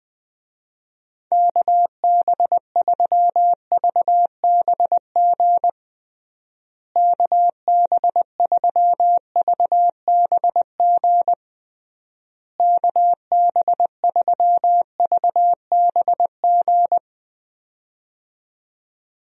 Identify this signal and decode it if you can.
Morse code, heard as a keyed tone, about 20 wpm, 700 Hz. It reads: KB3VBG KB3VBG KB3VBG
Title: USA Call Signs - 6 Characters - Repeated 3x - 20wpm